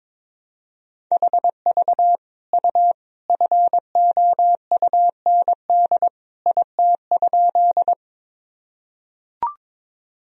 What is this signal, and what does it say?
Morse code, heard as a keyed tone, about 22 wpm, 700 Hz. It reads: HV U FOUND IT?
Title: Sentences from Top 300 Words and Abbreviations 22wpm